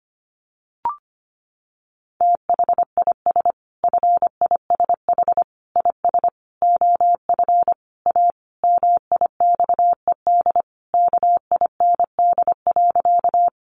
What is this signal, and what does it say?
Morse code, heard as a keyed tone, about 25 wpm, 700 Hz. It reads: THIS FISH IS OF A MIXED KIND.
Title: Sentences from Top 1500 Words 25wpm